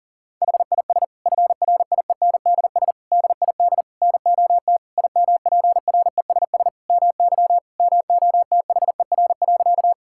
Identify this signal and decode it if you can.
Morse code, heard as a keyed tone, about 40 wpm, 700 Hz. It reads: HIS FRIENDS DID NOT IMPRESS MY MOTHER.
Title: Sentences from Top 1600 Words 40wpm